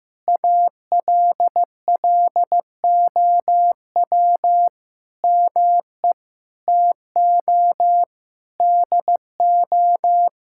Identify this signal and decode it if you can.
Morse code, heard as a keyed tone, about 15 wpm, 700 Hz. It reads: ALLOW ME TO DO
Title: Sentences from Top 1000 Words - Review Part 5 - 15wpm